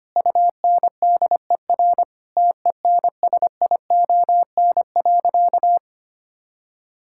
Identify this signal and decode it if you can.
Morse code, heard as a keyed tone, about 25 wpm, 700 Hz. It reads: UNDER TENSION.
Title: Sentences from Top 1600 Words 25wpm